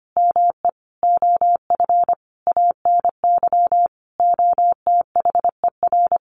Morse code, heard as a keyed tone, about 25 wpm, 700 Hz. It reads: ME OF ANY OTHER
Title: Sentences from Top 1400 Words 25wpm